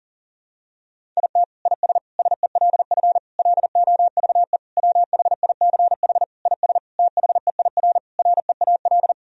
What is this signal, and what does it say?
Morse code, heard as a keyed tone, about 40 wpm, 700 Hz. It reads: IT IS SELF LOVE WHICH IS THEIR REAL
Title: Sentences from Top 1000 Words - Review Part 1 - 40wpm